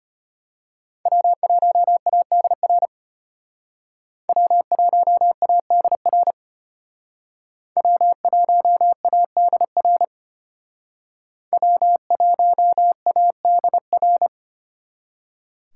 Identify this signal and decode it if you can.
Morse code, heard as a keyed tone, about 32 wpm, 700 Hz. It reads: W1ADR W1ADR W1ADR W1ADR